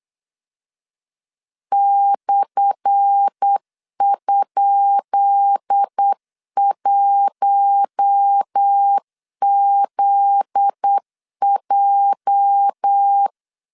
Morse code, heard as a keyed tone, about 8 wpm, 800 Hz. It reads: /?1ZJ
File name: LEZIONI / 10 SIGNS / 10C DOT 8wpm 3649